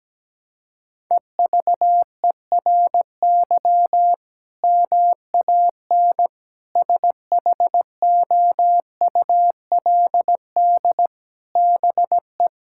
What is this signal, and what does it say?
Morse code, heard as a keyed tone, about 17 wpm, 700 Hz. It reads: EVERY MAN SHOULD BE
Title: Sentences from Top 1000 Words - Review Part 6 - 17wpm